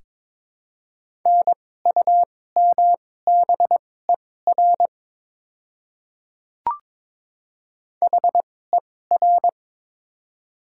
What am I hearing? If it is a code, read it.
Morse code, heard as a keyed tone, about 22 wpm, spaced as for 16 wpm, 700 Hz. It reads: NUMBER HER
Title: 2x Character Spacing - Top 100 Words - 22wpm